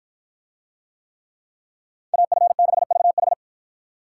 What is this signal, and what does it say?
Morse code, heard as a keyed tone, about 51 wpm, 700 Hz. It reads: AF6VH